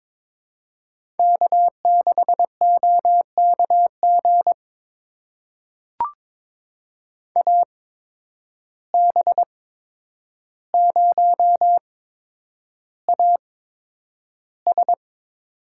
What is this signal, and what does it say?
Morse code, heard as a keyed tone, about 22 wpm, spaced as for 9 wpm, 700 Hz. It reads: K6OKG AB0AS
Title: USA Call Signs - 5 Characters - 8x Character Spacing - 22wpm